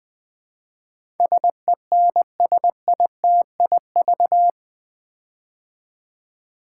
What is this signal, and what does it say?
Morse code, heard as a keyed tone, about 20 wpm, 700 Hz. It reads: SENSITIV
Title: Build A Word - 4 Letters At A Time - 11-Letter Words - 20wpm